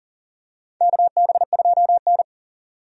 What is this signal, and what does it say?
Morse code, heard as a keyed tone, about 40 wpm, 690 Hz. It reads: KB2N
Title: USA Call Signs - 4 Characters - 40wpm